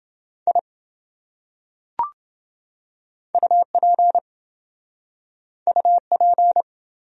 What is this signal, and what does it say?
Morse code, heard as a keyed tone, about 25 wpm, 700 Hz. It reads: I UP UP
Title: Speed-Racing Top 100 Words 20wpm